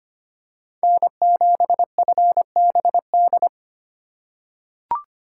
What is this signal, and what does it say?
Morse code, heard as a keyed tone, about 25 wpm, 700 Hz. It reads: N7FBD
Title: USA Call Signs - 5 Characters - 2x Character Spacing - 25wpm